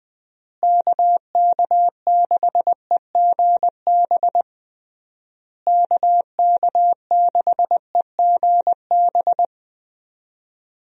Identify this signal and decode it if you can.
Morse code, heard as a keyed tone, about 20 wpm, 700 Hz. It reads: KK6EGB KK6EGB
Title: USA Call Signs - 6 Characters - Repeated 3x - 20wpm